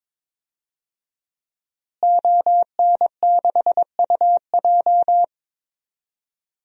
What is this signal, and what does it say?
Morse code, heard as a keyed tone, about 22 wpm, 700 Hz. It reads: ON6UJ
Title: DXCC-100 Call Signs - 5 Characters - Repeated 3x - 22wpm